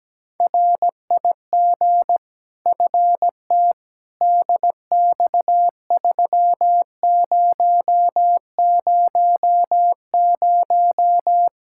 Morse code, heard as a keyed tone, about 17 wpm, 700 Hz. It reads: RIG FT DX3000
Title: K1USN SST QSO Practice 17wpm